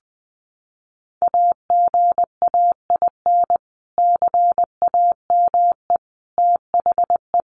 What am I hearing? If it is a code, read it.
Morse code, heard as a keyed tone, about 20 wpm, 700 Hz. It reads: AGAIN CAME THE
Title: Sentences from Top 500 Words 20wpm